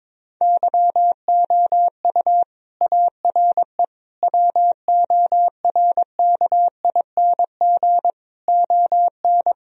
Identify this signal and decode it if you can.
Morse code, heard as a keyed tone, about 21 wpm, 700 Hz. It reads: YOU ARE WORKING ON
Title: Sentences from Top 1000 Words - Review Part 4 - 22wpm